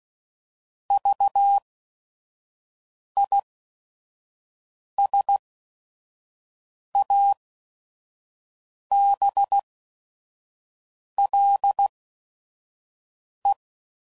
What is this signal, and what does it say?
Morse code, heard as a keyed tone, about 16 wpm, spaced as for 5 wpm, 800 Hz. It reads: VISABLE
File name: Prefixes / By First Letter / Vs